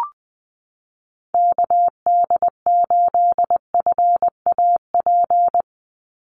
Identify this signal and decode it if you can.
Morse code, heard as a keyed tone, about 20 wpm, 700 Hz. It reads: KD8FAP
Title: USA Call Signs 20wpm